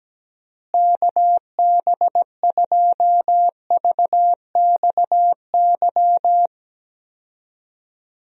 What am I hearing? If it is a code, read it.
Morse code, heard as a keyed tone, about 17 wpm, 700 Hz. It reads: KB2VXY
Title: USA Call Signs - 6 Characters - Repeated 3x - 17wpm